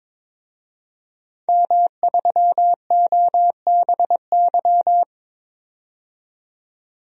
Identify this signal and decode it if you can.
Morse code, heard as a keyed tone, about 23 wpm, 700 Hz. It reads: M3OBY
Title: DXCC-100 Call Signs - 5 Characters - 22wpm